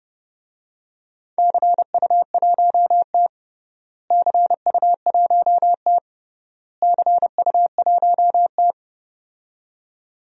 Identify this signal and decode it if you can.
Morse code, heard as a keyed tone, about 32 wpm, 700 Hz. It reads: CU1T CU1T CU1T
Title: DXCC-100 Call Signs - 4 Characters - Repeated 3x - 30wpm